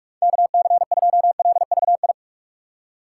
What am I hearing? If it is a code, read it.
Morse code, heard as a keyed tone, about 44 wpm, 700 Hz. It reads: KC2LVI
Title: USA Call Signs - 6x Character Spacing - 45wpm